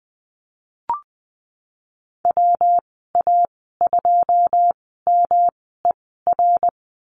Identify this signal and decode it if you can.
Morse code, heard as a keyed tone, about 20 wpm, spaced as for 14 wpm, 700 Hz. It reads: WA2MER
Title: 128 Lesson 21g - Number 6 - Call Signs 20wpm